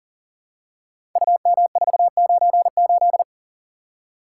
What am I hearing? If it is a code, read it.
Morse code, heard as a keyed tone, about 41 wpm, 690 Hz. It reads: UM498
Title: Code Groups - 5 Letter:Numbers - 40wpm